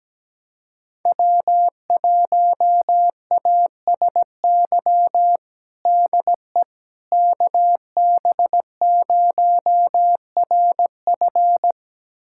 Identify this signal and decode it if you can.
Morse code, heard as a keyed tone, about 17 wpm, 700 Hz. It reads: W1ASY DE KB0RF
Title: USA Callsign DE Callsign - 5 Characters - 17wpm